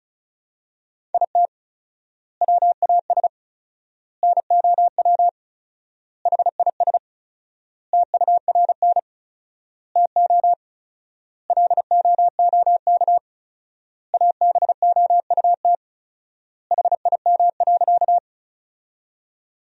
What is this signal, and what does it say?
Morse code, heard as a keyed tone, about 36 wpm, 700 Hz. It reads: IT WAS NOW HIS TURN TO LOOK ABOUT HIM.
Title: 4x Word-Spacing - Sentences from Top 200 Words - 35wpm